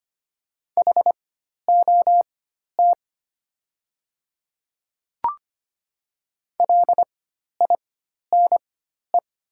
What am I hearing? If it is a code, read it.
Morse code, heard as a keyed tone, about 25 wpm, spaced as for 12 wpm, 700 Hz. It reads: HOT LINE
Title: 4x Character Spacing - Top 200 Words - 25wpm